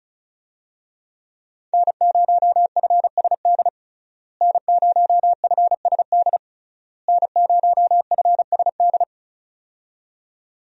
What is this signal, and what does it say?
Morse code, heard as a keyed tone, about 36 wpm, 700 Hz. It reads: N0FSD N0FSD N0FSD
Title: USA Call Signs - 5 Characters - Repeated 3x - 35wpm